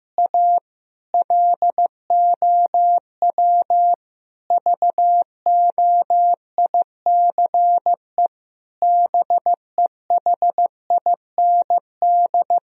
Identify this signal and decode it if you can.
Morse code, heard as a keyed tone, about 15 wpm, 700 Hz. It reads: A LOW VOICE BEHIND
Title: Sentences from Top 1000 Words - Review Part 1 - 15wpm